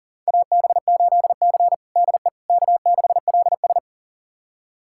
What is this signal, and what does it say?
Morse code, heard as a keyed tone, about 40 wpm, 700 Hz. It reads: AB8C DE K6LS
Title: USA Callsign DE Callsign - 4 Characters - 40wpm